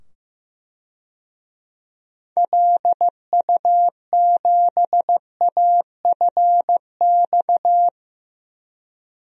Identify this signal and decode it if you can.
Morse code, heard as a keyed tone, about 15 wpm, 700 Hz. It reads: LU7AFX
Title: DXCC-100 Call Signs - 6 Characters - 15wpm